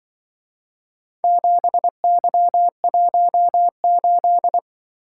Morse code, heard as a keyed tone, about 24 wpm, 700 Hz. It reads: 7Y18
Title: Speed-Racing - Four Letter-Number Combinations - 20wpm